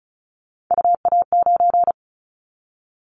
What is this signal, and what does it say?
Morse code, heard as a keyed tone, about 34 wpm, 700 Hz. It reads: UA9
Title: Callsign Prefixes for Top 50 DXCC - 35wpm